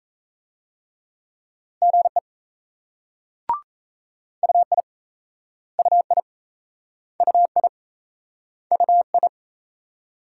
Morse code, heard as a keyed tone, about 36 wpm, 700 Hz. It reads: GE UI UI UI UI